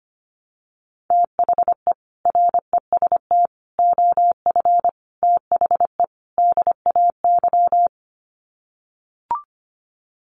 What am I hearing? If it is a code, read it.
Morse code, heard as a keyed tone, about 25 wpm, 700 Hz. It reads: THE REST OF THE DAY
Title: Sets of 5 Words 25wpm